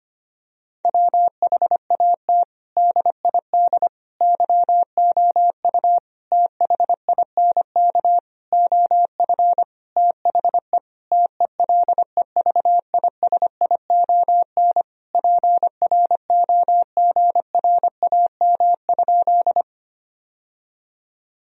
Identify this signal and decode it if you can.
Morse code, heard as a keyed tone, about 25 wpm, 700 Hz. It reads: WHAT DID YOU THINK OF THE TELEVISION PROGRAM?
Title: Sentences from Top 1000 Words - Review Part 4 - 25wpm